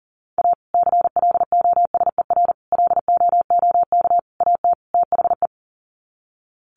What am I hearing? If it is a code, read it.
Morse code, heard as a keyed tone, about 40 wpm, 700 Hz. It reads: A CLOSER LOOK AT THE
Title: Sets of 5 Words Encore 40wpm